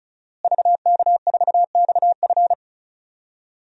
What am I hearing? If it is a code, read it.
Morse code, heard as a keyed tone, about 36 wpm, 700 Hz. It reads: VK4XF
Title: DXCC-100 Call Signs - 5 Characters - Repeated 2x - 35wpm